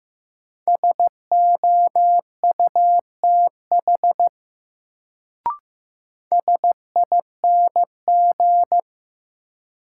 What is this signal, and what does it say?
Morse code, heard as a keyed tone, about 16 wpm, 700 Hz. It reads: SOUTH SING
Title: Sentences from Top 400 Words - New Word Drill - 15wpm